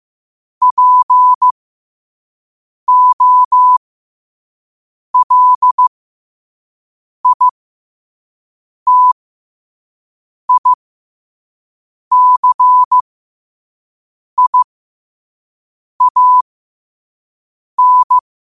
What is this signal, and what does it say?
Morse code, heard as a keyed tone, about 15 wpm, spaced as for 5 wpm, 1000 Hz. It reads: POLITICIAN
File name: qotdfiles / 5wpm